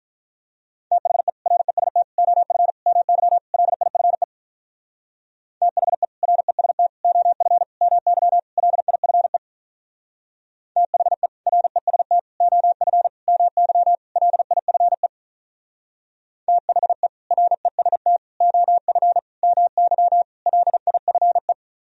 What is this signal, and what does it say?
Morse code, heard as a keyed tone, about 44 wpm, 700 Hz. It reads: THE REST OF MY LIFE THE REST OF MY LIFE THE REST OF MY LIFE THE REST OF MY LIFE